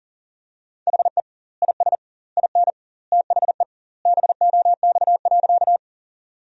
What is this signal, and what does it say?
Morse code, heard as a keyed tone, about 40 wpm, 700 Hz. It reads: HE IS IN THE BOX.
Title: 2x Word-Spacing - Sentences from Top 500 Words - 40wpm